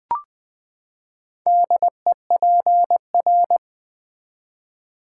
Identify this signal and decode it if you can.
Morse code, heard as a keyed tone, about 20 wpm, 700 Hz. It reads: DEPR
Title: Build A Word - 4 Letters At A Time - 10-Letter Words - 20wpm